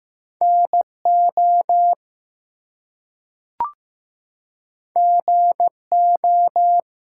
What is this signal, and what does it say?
Morse code, heard as a keyed tone, about 15 wpm, 700 Hz. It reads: NO GO
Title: Two Letter Words - 15wpm